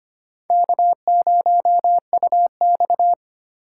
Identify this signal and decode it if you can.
Morse code, heard as a keyed tone, about 25 wpm, 700 Hz. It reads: K0UX
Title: USA Call Signs - 4 Characters - 3x Character Spacing - 25wpm